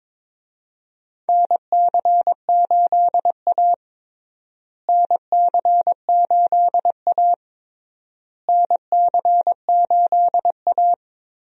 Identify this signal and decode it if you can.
Morse code, heard as a keyed tone, about 22 wpm, 700 Hz. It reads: NC8A NC8A NC8A
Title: USA Call Signs - 4 Characters - Repeated 3x - 22wpm